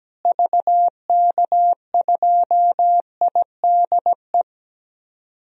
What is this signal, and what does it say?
Morse code, heard as a keyed tone, about 17 wpm, 700 Hz. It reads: VK2IDE
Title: DXCC-100 Call Signs - 6 Characters - Repeated 2x - 17wpm